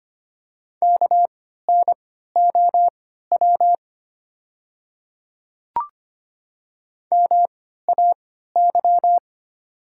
Morse code, heard as a keyed tone, about 25 wpm, spaced as for 14 wpm, 700 Hz. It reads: KNOW MAY
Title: 3x Character Spacing - Top 100 Words - 25wpm